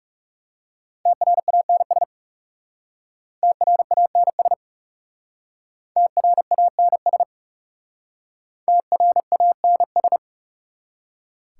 Morse code, heard as a keyed tone, about 39 wpm, 700 Hz. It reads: TRANS TRANS TRANS TRANS